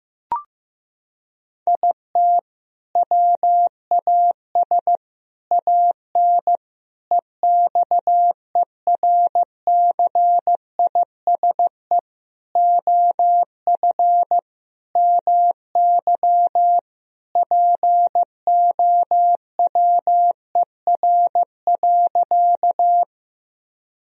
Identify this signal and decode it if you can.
Morse code, heard as a keyed tone, about 15 wpm, 700 Hz. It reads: IT WAS AN EXERCISE OF MY POWER.
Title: Sentences from Top 1000 Words - Review Part 4 - 15wpm